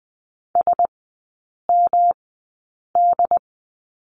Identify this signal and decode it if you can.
Morse code, heard as a keyed tone, about 20 wpm, spaced as for 8 wpm, 700 Hz. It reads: SMD
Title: 114 Lesson 19e - Letter w - Three Characters 20wpm